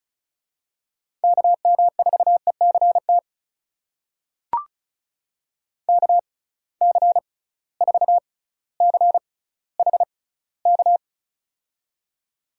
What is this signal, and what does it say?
Morse code, heard as a keyed tone, about 36 wpm, spaced as for 18 wpm, 700 Hz. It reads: KM4ECT KC4CHK
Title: USA Call Signs - 6 Characters - 6x Character Spacing - 35wpm